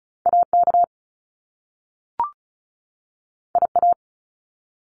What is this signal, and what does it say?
Morse code, heard as a keyed tone, about 36 wpm, 700 Hz. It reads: AK IA
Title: US States 35wpm